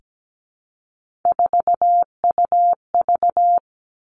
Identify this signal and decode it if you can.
Morse code, heard as a keyed tone, about 17 wpm, 700 Hz. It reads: 4UV
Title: Three Challenging Letter-Number Combinations 17wpm